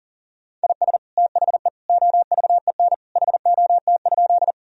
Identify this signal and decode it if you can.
Morse code, heard as a keyed tone, about 40 wpm, 700 Hz. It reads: IS THE OVEN HOT?
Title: Sentences from Top 1000 Words - Review Part 4 - 40wpm